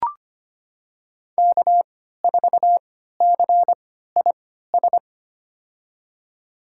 Morse code, heard as a keyed tone, about 25 wpm, spaced as for 14 wpm, 700 Hz. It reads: K4CIS
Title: USA Call Signs - 5 Characters - 3x Character Spacing - 25wpm